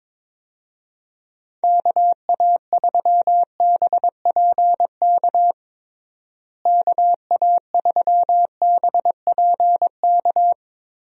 Morse code, heard as a keyed tone, about 22 wpm, 700 Hz. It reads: KA3BPK KA3BPK
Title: USA Call Signs - 6 Characters - Repeated 3x - 22wpm